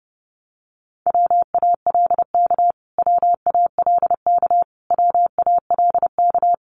Morse code, heard as a keyed tone, about 30 wpm, 700 Hz. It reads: WALK WALK WALK
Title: Top 500 Words Repeated 30wpm